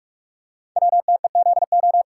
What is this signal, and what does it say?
Morse code, heard as a keyed tone, about 47 wpm, 700 Hz. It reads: WTEZO
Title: Code Groups - 5 Letters - 45wpm